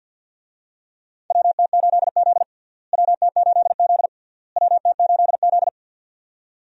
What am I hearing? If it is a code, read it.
Morse code, heard as a keyed tone, about 48 wpm, 700 Hz. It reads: WT8Z WT8Z WT8Z
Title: USA Call Signs - 4 Characters - Repeated 3x - 50wpm